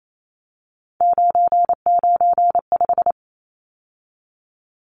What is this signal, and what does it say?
Morse code, heard as a keyed tone, about 28 wpm, 700 Hz. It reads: 995
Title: Numbers - 3-Digit 28wpm